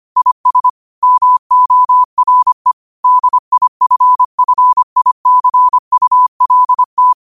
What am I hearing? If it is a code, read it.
Morse code, heard as a keyed tone, about 25 wpm, 1000 Hz. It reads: IS MORE DIFFICULT